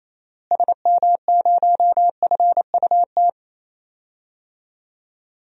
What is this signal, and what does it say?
Morse code, heard as a keyed tone, about 27 wpm, 700 Hz. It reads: SM0FUT
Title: DXCC-100 Call Signs - 6 Characters - Repeated 2x - 28wpm